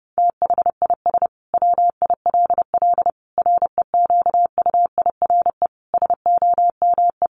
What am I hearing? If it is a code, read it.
Morse code, heard as a keyed tone, about 28 wpm, 700 Hz. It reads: THIS WILL REQUIRE SOME
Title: Sentences from Top 900 Words 30wpm